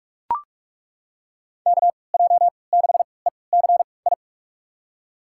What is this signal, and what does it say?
Morse code, heard as a keyed tone, about 42 wpm, spaced as for 25 wpm, 700 Hz. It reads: KJ6ECI
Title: USA Call Signs - 6 Characters - 3x Character Spacing - 45wpm